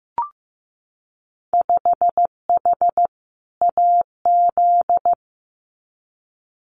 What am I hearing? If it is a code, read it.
Morse code, heard as a keyed tone, about 15 wpm, 700 Hz. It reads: 5H AZ
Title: Winter Field Day Exchanges Encore 15wpm